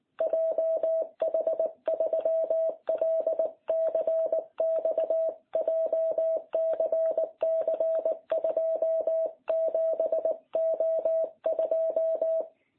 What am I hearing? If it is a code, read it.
Morse code, heard as a keyed tone, about 19 wpm, 660 Hz. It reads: JH3LCX JCC 27O2